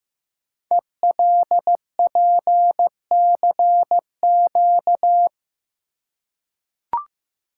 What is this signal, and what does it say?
Morse code, heard as a keyed tone, about 15 wpm, 700 Hz. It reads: ELPCQ
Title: Code Groups - 5 Letters - 15wpm